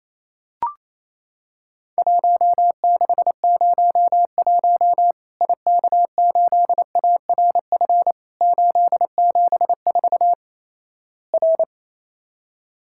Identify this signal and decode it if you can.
Morse code, heard as a keyed tone, about 28 wpm, 700 Hz. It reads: 1601 IK8ARF 874 R
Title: Worked All Europe DX Contest - QTC Exchange - 28wpm